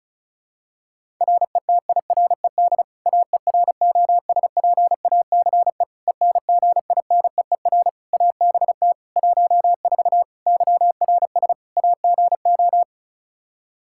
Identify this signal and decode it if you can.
Morse code, heard as a keyed tone, about 34 wpm, 700 Hz. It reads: RETIRED AEROSPACE ENGINEER ABT 14 YRS AGO
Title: K1USN SST QSO Practice 35wpm